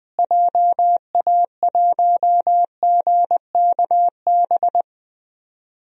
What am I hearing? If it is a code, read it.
Morse code, heard as a keyed tone, about 20 wpm, 700 Hz. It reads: JA1GKB
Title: DXCC-100 Call Signs - 6 Characters - Repeated 2x - 20wpm